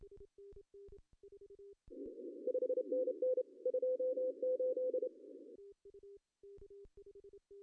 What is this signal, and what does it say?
Morse code, heard as a keyed tone, about 27 wpm, 510 Hz. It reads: 5NN 28
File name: CW / cqww-cw-k5tr